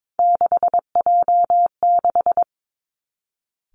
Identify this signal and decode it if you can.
Morse code, heard as a keyed tone, about 22 wpm, 700 Hz. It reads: 6J6
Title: Three Challenging Letter-Number Combinations 22wpm